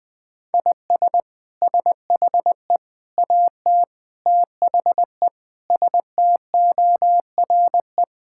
Morse code, heard as a keyed tone, about 20 wpm, 700 Hz. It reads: IS SHE AT THE STORE
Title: Sentences from Top 1000 Words - Review Part 5 - 20wpm